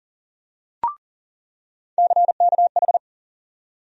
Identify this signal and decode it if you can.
Morse code, heard as a keyed tone, about 40 wpm, 700 Hz. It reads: CKH
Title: USA Call Sign Postfixes - 3 Letters 40wpm